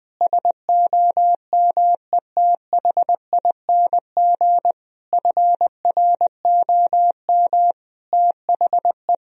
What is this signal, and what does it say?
Morse code, heard as a keyed tone, about 20 wpm, 700 Hz. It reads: SOMETHING FROM THE
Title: Sentences from Top 1000 Words - Review Part 4 - 20wpm